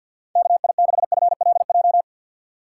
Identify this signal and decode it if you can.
Morse code, heard as a keyed tone, about 52 wpm, 700 Hz. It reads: KI6FLJ
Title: USA Call Signs - 6 Characters - 6x Character Spacing - 50wpm